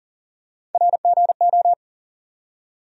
Morse code, heard as a keyed tone, about 40 wpm, 700 Hz. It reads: RGO
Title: USA Call Sign Postfixes - 3 Letters 40wpm